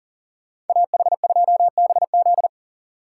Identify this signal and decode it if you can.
Morse code, heard as a keyed tone, about 49 wpm, 700 Hz. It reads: AH2BZ
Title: Speed Racing - USA 2x2 Call Signs 40wpm